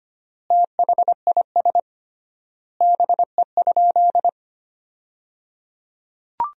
Morse code, heard as a keyed tone, about 25 wpm, 700 Hz. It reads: THIS BE?
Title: 3x Word-Spacing - Sentences from Top 100 Words - 25wpm